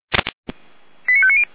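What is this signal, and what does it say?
Five-tone selective call (ZVEI), heard as quick runs of five tones, tones of about 70 ms. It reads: 89590
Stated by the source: ZVEI five-tone selective call received with an SDR.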